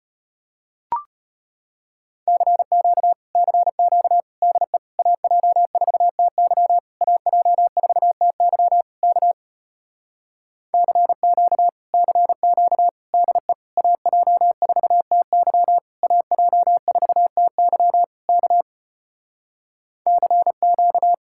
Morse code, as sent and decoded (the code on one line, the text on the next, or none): -.-. --.- / -.-. --.- / -.. / .- .--- ....- - -.-- / .- .--- ....- - -.-- / -.- / -.-. --.- / -.-. --.- / -.. / .- .--- ....- - -.-- / .- .--- ....- - -.-- / -.- / -.-. --.-
CQ CQ DE AJ4TY AJ4TY K CQ CQ DE AJ4TY AJ4TY K CQ